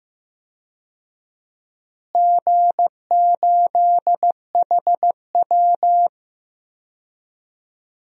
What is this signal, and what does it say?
Morse code, heard as a keyed tone, about 15 wpm, 700 Hz. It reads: G8HW
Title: DXCC-100 Call Signs - 4 Characters - Repeated 2x - 15wpm